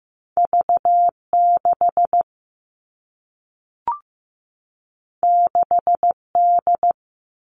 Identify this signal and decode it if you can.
Morse code, heard as a keyed tone, about 15 wpm, 700 Hz. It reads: V6 6D
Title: Two Challenging Letter-Number Combinations 15wpm